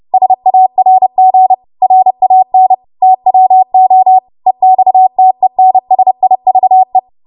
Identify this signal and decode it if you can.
Morse code, heard as a keyed tone, about 30 wpm, 750 Hz. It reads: SARG RAN TWO EXTENSIVE
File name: Morse / 30WPM